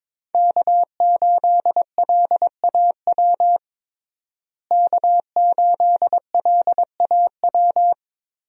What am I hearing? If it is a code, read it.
Morse code, heard as a keyed tone, about 22 wpm, 700 Hz. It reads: K8LAW K8LAW
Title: USA Call Signs - 5 Characters - Repeated 3x - 22wpm